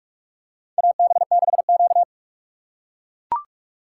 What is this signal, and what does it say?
Morse code, heard as a keyed tone, about 44 wpm, 700 Hz. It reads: AB6Q
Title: USA Call Signs - 4x Character Spacing - 45wpm